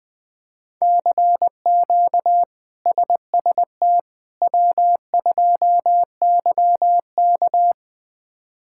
Morse code, heard as a keyed tone, about 20 wpm, 700 Hz. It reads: CQ SST W2YK
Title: K1USN SST QSO Practice - 20wpm